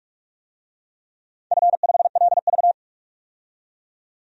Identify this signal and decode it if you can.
Morse code, heard as a keyed tone, about 44 wpm, 700 Hz. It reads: F5LV
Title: DXCC-100 Call Signs - 4 Characters - 45wpm